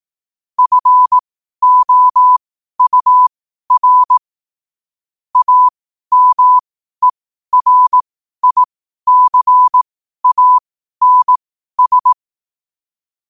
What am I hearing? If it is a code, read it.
Morse code, heard as a keyed tone, about 18 wpm, spaced as for 12 wpm, 1000 Hz. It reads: FOUR AMERICANS
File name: qotdfiles / 12wpm